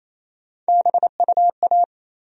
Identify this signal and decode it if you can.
Morse code, heard as a keyed tone, about 28 wpm, 700 Hz. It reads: BUA
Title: USA Call Sign Postfixes - 3 Letters 28wpm